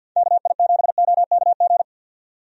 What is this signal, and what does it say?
Morse code, heard as a keyed tone, about 51 wpm, 700 Hz. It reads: KI7OKG
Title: USA Call Signs - 6 Characters - 50wpm